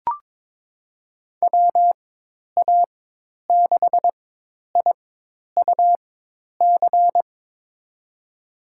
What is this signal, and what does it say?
Morse code, heard as a keyed tone, about 22 wpm, spaced as for 10 wpm, 700 Hz. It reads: WA6IUC
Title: USA Call Signs - 6 Characters - 4x Character Spacing - 22wpm